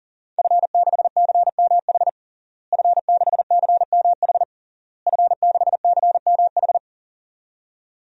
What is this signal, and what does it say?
Morse code, heard as a keyed tone, about 40 wpm, 700 Hz. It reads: F6CMH F6CMH F6CMH
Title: DXCC-100 Call Signs - 5 Characters - Repeated 3x - 40wpm